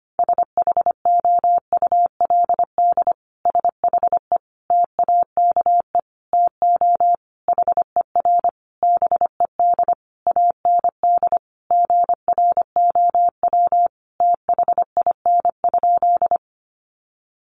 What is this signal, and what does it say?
Morse code, heard as a keyed tone, about 25 wpm, 700 Hz. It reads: SHOULD SHE TAKE TO HER BED AND GROW THIN?
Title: Sentences from Top 1200 Words 25wpm